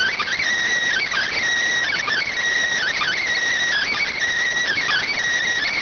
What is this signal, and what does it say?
Emergency Alert System (SAME) tones, heard as harsh buzzing bursts. It is heard from 0.9 to 5.1 s.
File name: findings / CT 205 ax mips, iDose 3 0004.gen-32 rx